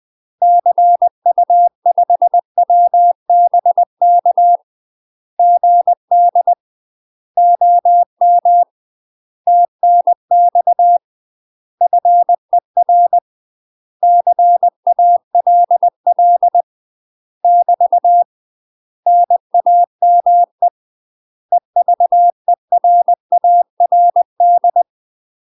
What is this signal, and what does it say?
Morse code, heard as a keyed tone, about 20 wpm, 700 Hz. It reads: CU5WBK GD OM TNX FER CALL <BT> NAME EVERARD
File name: UK MASTER QSOs TRAINING / 20WPM / UK MASTER QSOs 0038